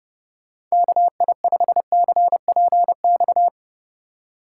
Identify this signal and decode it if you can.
Morse code, heard as a keyed tone, about 35 wpm, 700 Hz. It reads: KI5CPX
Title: Speed Racing - USA Call Signs 30wpm